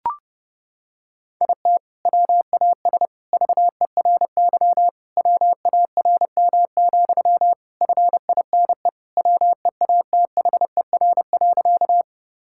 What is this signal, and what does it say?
Morse code, heard as a keyed tone, about 29 wpm, 700 Hz. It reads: IT WAS VERY WARM, FINE WEATHER.
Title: Sentences from Top 1000 Words - Review Part 5 - 30wpm